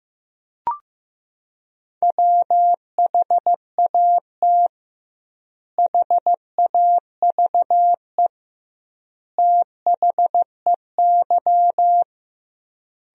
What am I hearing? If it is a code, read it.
Morse code, heard as a keyed tone, about 15 wpm, 700 Hz. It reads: WHAT HAVE THEY
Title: 2x Word-Spacing - Sentences from Top 500 Words - 15wpm